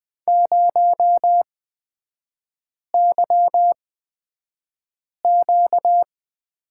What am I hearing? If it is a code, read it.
Morse code, heard as a keyed tone, about 20 wpm, spaced as for 5 wpm, 700 Hz. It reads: 0YQ
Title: 236 Lesson 38e - Letter q - Three Characters 20wpm